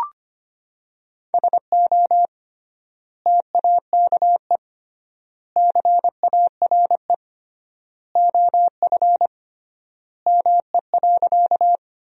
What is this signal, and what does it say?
Morse code, heard as a keyed tone, about 25 wpm, 700 Hz. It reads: SO TAKE CARE OF ME.
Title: 3x Word-Spacing - Sentences from Top 300 Words - 25wpm